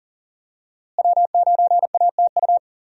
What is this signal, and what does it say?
Morse code, heard as a keyed tone, about 40 wpm, 700 Hz. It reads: W9ATU
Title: USA Call Signs - 5 Characters - Repeated 3x - 40wpm